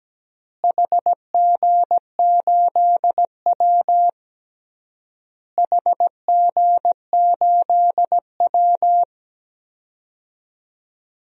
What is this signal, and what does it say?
Morse code, heard as a keyed tone, about 17 wpm, 700 Hz. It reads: HG8W HG8W
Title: DXCC-100 Call Signs - 4 Characters - Repeated 3x - 17wpm